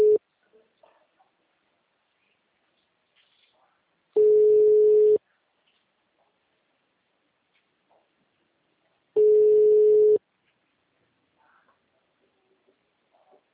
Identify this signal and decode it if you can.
Ringback tone (Europe, 425 Hz).
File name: out- -2019- - - .1406